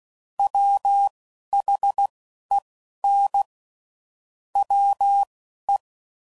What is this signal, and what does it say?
Morse code, heard as a keyed tone, about 16 wpm, spaced as for 11 wpm, 800 Hz. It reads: WHEN WE
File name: Sayings / Quotes / Deceive